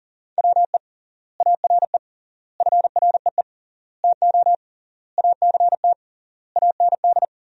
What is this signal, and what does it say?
Morse code, heard as a keyed tone, about 40 wpm, 700 Hz. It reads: WE ARE FREE TO ACT AND
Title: 3x Word-Spacing - Sentences from Top 500 Words - 40wpm